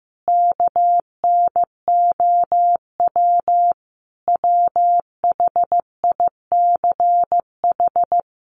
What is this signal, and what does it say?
Morse code, heard as a keyed tone, about 14 wpm, 700 Hz. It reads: KNOW WHICH
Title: A Sound Of Thunder 15wpm